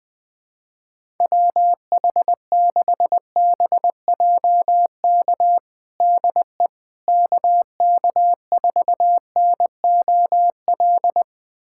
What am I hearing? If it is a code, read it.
Morse code, heard as a keyed tone, about 21 wpm, 700 Hz. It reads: WH6BJK DE KK4NOL
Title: USA Callsign DE Callsign 20wpm